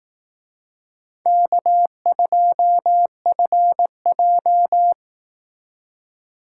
Morse code, heard as a keyed tone, about 18 wpm, 700 Hz. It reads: K2FJ
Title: Speed Racing - USA 1x2 Call Signs 15wpm